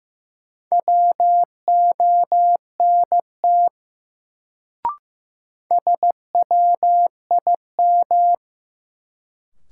Morse code, heard as a keyed tone, about 16 wpm, 700 Hz. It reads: WONT SWIM
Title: Sentences from Top 1000 Words - Review Part 6 - New Word Drill - 15wpm